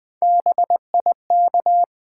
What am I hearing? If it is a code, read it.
Morse code, heard as a keyed tone, about 20 wpm, 700 Hz. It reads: BIK
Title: Build A Word - 1 Letter At A Time - 4-Letter Words - 20wpm